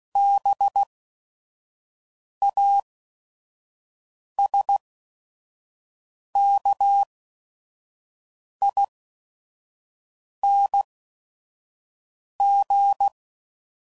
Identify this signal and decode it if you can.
Morse code, heard as a keyed tone, about 16 wpm, spaced as for 5 wpm, 800 Hz. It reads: BASKING
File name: Suffixes / Special ING's / VerySlowToFaster ING's